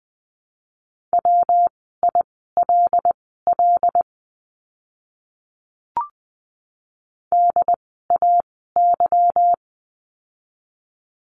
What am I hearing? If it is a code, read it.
Morse code, heard as a keyed tone, about 20 wpm, spaced as for 15 wpm, 700 Hz. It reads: WILL DAY
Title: 244 Lesson 39f - Letter z - Words 20wpm